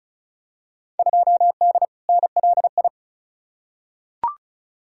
Morse code, heard as a keyed tone, about 34 wpm, 700 Hz. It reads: 2D NLI
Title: Field Day Exchanges - 35wpm